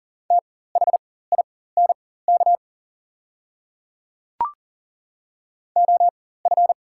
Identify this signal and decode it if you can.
Morse code, heard as a keyed tone, about 40 wpm, spaced as for 19 wpm, 700 Hz. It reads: THINK OF
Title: 4x Character Spacing - Top 200 Words - 40wpm